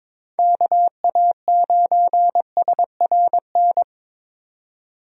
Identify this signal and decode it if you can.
Morse code, heard as a keyed tone, about 22 wpm, 700 Hz. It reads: KA9SRN
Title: USA Call Signs - 6 Characters - 22wpm